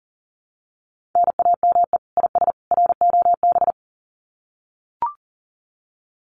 Morse code, heard as a keyed tone, about 40 wpm, 700 Hz. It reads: NAME IS ROB
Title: Common QSO Phrases 40wpm